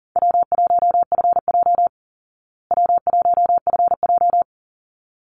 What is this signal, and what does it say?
Morse code, heard as a keyed tone, about 40 wpm, 700 Hz. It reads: W1FJ W1FJ
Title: Easy Call Signs Repeated 2x - 40wpm